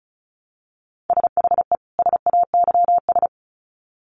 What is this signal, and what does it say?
Morse code, heard as a keyed tone, about 34 wpm, 700 Hz. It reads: SHE SAYS
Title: Sets of 2 Words 35wpm